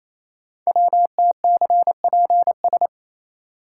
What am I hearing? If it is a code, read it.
Morse code, heard as a keyed tone, about 29 wpm, 700 Hz. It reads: WTCPS
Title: Code Groups - 5 Letters - 28wpm